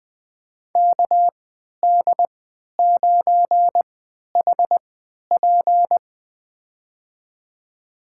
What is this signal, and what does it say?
Morse code, heard as a keyed tone, about 21 wpm, spaced as for 12 wpm, 700 Hz. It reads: KD9HP
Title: USA Call Signs - 5 Characters - 3x Character Spacing - 20wpm